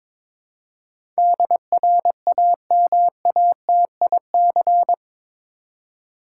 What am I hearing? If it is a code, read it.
Morse code, heard as a keyed tone, about 21 wpm, 700 Hz. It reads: DRAMATIC
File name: practice / 22wpm / Eight Letter Words - 22wpm